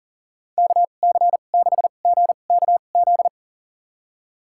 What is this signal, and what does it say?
Morse code, heard as a keyed tone, about 40 wpm, spaced as for 29 wpm, 700 Hz. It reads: KC6GKZ
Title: USA Call Signs - 2x Character Spacing - 40wpm